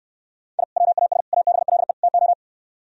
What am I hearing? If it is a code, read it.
Morse code, heard as a keyed tone, about 67 wpm, 700 Hz. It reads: I WAS ABLE TO